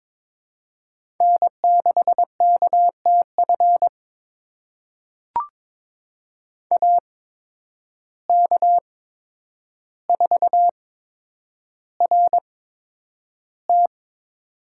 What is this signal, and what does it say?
Morse code, heard as a keyed tone, about 22 wpm, spaced as for 9 wpm, 700 Hz. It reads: N6KTF AK4RT
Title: USA Call Signs - 5 Characters - 8x Character Spacing - 22wpm